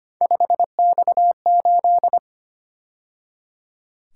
Morse code, heard as a keyed tone, about 29 wpm, 700 Hz. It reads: 5X8
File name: Speed-Racing - Three Letter-Number Combinations - 25wpm